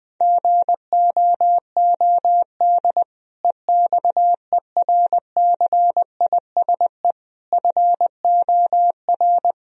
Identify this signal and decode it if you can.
Morse code, heard as a keyed tone, about 20 wpm, 700 Hz. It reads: GOOD EXERCISE FOR
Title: Sentences from Top 1000 Words - Review Part 4 - 20wpm